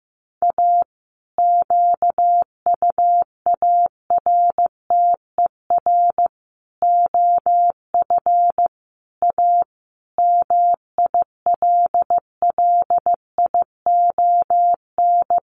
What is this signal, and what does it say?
Morse code, heard as a keyed tone, about 15 wpm, 700 Hz. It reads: A QUARTER OF A MILLION
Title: Sets of 5 Words Encore2 15wpm